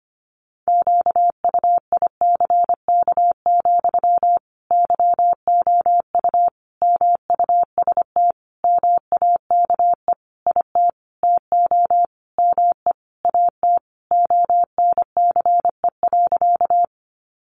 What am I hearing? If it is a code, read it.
Morse code, heard as a keyed tone, about 25 wpm, 700 Hz. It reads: QUICK, YOU MUST MAKE IT TO ME AT ONCE.
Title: Sentences from Top 500 Words 25wpm